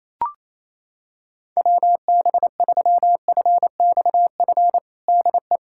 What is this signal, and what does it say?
Morse code, heard as a keyed tone, about 28 wpm, 700 Hz. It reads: WB3FXF DE
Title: USA Callsign DE Callsign 28wpm